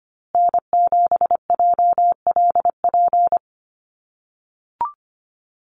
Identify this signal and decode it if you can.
Morse code, heard as a keyed tone, about 25 wpm, 700 Hz. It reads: N7JLP
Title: USA Call Signs Repeated 2x 25wpm